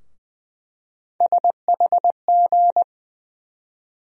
Morse code, heard as a keyed tone, about 20 wpm, 700 Hz. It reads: SHG
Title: USA Call Sign Postfixes - 3 Letters 20wpm